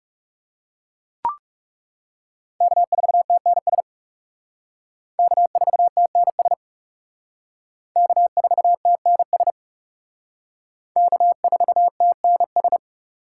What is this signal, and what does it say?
Morse code, heard as a keyed tone, about 37 wpm, 700 Hz. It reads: K4TNS K4TNS K4TNS K4TNS
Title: Speed Racing - USA 1x3 Call Signs 30wpm